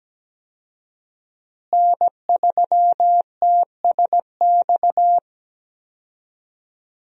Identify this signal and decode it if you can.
Morse code, heard as a keyed tone, about 17 wpm, 700 Hz. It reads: N3TSX
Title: USA Call Signs - 5 Characters - Repeated 2x - 17wpm